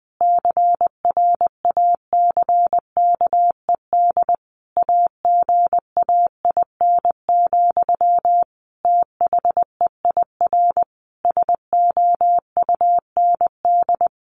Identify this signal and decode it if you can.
Morse code, heard as a keyed tone, about 20 wpm, 700 Hz. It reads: CRACKED AGAIN, THEIR SOUND
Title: A Sound Of Thunder 20wpm